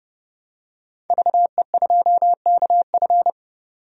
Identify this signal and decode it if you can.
Morse code, heard as a keyed tone, about 30 wpm, 700 Hz. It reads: VE2KF
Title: DXCC-100 Call Signs - 5 Characters - 30wpm